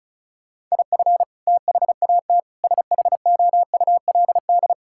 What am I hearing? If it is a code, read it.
Morse code, heard as a keyed tone, about 36 wpm, 700 Hz. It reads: IF THAT SHOULD
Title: Sentences from Top 1000 Words - Review Part 6 - 35wpm